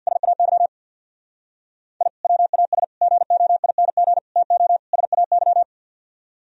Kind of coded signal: Morse code

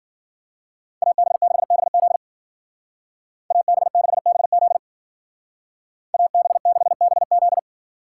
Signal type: Morse code

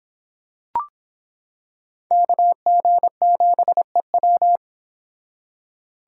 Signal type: Morse code

